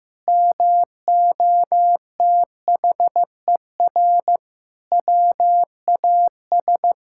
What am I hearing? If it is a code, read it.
Morse code, heard as a keyed tone, about 15 wpm, 700 Hz. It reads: MOTHER WAS
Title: Sentences from Top 1000 Words - Review Part 5 - 15wpm